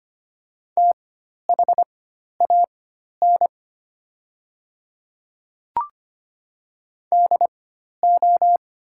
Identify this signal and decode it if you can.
Morse code, heard as a keyed tone, about 25 wpm, spaced as for 12 wpm, 700 Hz. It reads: THAN DO
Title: 4x Character Spacing - Top 100 Words - 25wpm